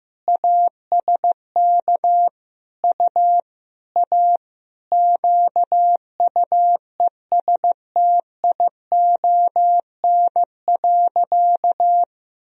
Morse code, heard as a keyed tone, about 15 wpm, 700 Hz. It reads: ASK U A QUESTION.
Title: Sentences from Top 400 Words and Abbreviations 15wpm